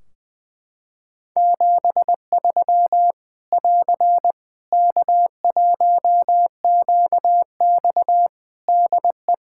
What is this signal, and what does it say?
Morse code, heard as a keyed tone, about 20 wpm, 700 Hz. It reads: 73 <AR> K1QX DE
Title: Practice QSO - Exchange 4 - Ending - 20wpm